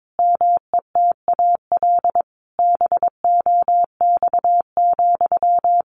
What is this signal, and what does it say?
Morse code, heard as a keyed tone, about 22 wpm, 700 Hz. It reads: METAL BOX,
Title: A Sound Of Thunder 22wpm